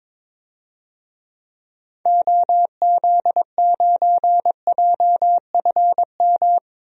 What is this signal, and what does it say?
Morse code, heard as a keyed tone, about 22 wpm, 700 Hz. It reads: OZ9JFM
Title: DXCC-100 Call Signs - 6 Characters - Repeated 2x - 22wpm